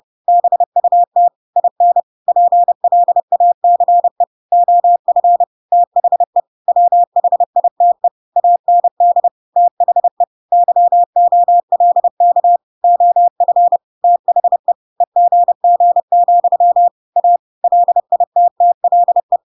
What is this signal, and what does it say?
Morse code, heard as a keyed tone, about 30 wpm, 700 Hz. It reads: BUT IN PLACE OF THE WHITE AND THE YOLK OF THE EGG, A LITTLE
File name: Pinocchio / Pinocchio ch5 30 30 ik1hge